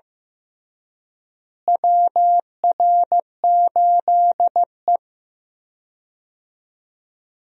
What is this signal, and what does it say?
Morse code, heard as a keyed tone, about 15 wpm, 700 Hz. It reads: WR8E
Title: USA Call Signs - 4 Characters - Repeated 3x - 15wpm